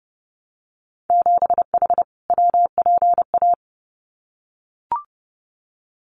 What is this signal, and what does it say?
Morse code, heard as a keyed tone, about 30 wpm, 700 Hz. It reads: 7H WPA
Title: Winter Field Day Exchanges Encore 30wpm